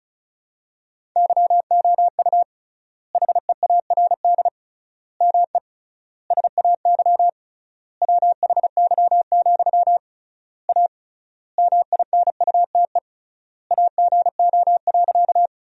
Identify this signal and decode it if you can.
Morse code, heard as a keyed tone, about 34 wpm, 700 Hz. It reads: YOU HEARD ME SAY WHY, A MINUTE AGO.
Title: 3x Word-Spacing - Sentences from Top 500 Words - 35wpm